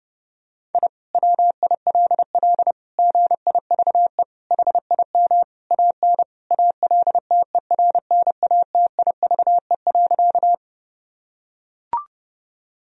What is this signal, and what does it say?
Morse code, heard as a keyed tone, about 30 wpm, 700 Hz. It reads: I WILL GIVE HIM AN ALTERNATIVE.
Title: Sentences from Top 1000 Words - Review Part 4 - 30wpm